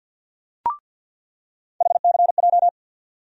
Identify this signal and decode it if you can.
Morse code, heard as a keyed tone, about 46 wpm, 700 Hz. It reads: HCJ
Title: USA Call Sign Postfixes - 3 Letters 50wpm